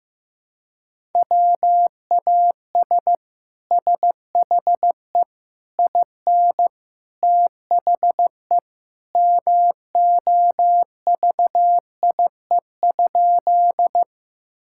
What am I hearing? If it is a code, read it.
Morse code, heard as a keyed tone, about 15 wpm, 700 Hz. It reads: WAS SHE IN THE MOVIE?
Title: Sentences from Top 1000 Words - Review Part 5 - 15wpm